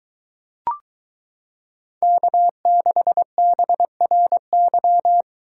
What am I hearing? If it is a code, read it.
Morse code, heard as a keyed tone, about 22 wpm, 700 Hz. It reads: K6BRY